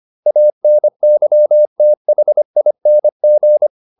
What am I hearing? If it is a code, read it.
Morse code, heard as a keyed tone, about 25 wpm, 600 Hz. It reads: ANYTHING
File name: SS220 25